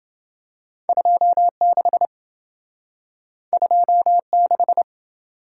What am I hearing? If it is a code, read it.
Morse code, heard as a keyed tone, about 27 wpm, 700 Hz. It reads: 26 26